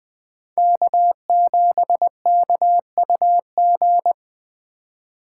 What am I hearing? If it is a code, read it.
Morse code, heard as a keyed tone, about 20 wpm, 700 Hz. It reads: K7KUG